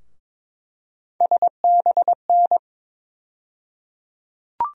Morse code, heard as a keyed tone, about 22 wpm, 700 Hz. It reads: SBN
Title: USA Call Sign Postfixes - 3 Letters 22wpm